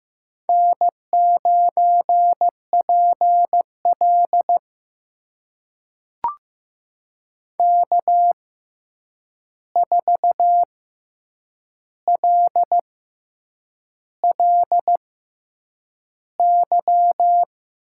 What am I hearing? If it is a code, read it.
Morse code, heard as a keyed tone, about 15 wpm, spaced as for 7 wpm, 700 Hz. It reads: N9PL K4LLY
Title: USA Call Signs - 6x Character Spacing - 15wpm